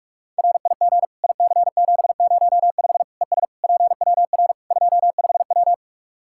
Morse code, heard as a keyed tone, about 41 wpm, 700 Hz. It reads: RIG IC705 ES PWR 25W
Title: Practice QSO - Exchange 2 - Info - 45wpm